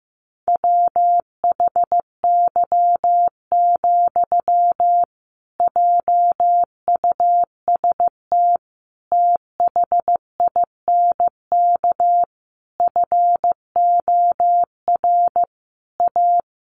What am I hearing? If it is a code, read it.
Morse code, heard as a keyed tone, about 15 wpm, 700 Hz. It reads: WHY, JUST THINK FOR A
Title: Sentences from Top 1000 Words 15wpm